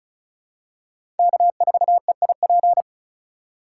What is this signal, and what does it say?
Morse code, heard as a keyed tone, about 34 wpm, 700 Hz. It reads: K4EIP
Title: USA Call Signs - 5 Characters - 4x Character Spacing - 35wpm